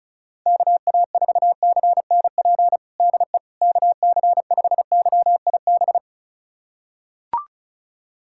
Morse code, heard as a keyed tone, about 34 wpm, 700 Hz. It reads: KA4CNP DE KC5YIB
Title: USA Callsign DE Callsign - 6 Characters - 35wpm